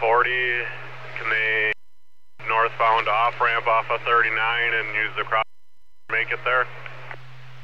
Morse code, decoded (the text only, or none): none